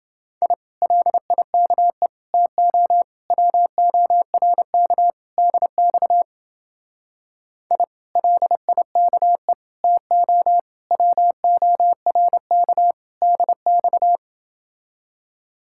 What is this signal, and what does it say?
Morse code, heard as a keyed tone, about 29 wpm, 700 Hz. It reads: I LIKE TO WORK DX I LIKE TO WORK DX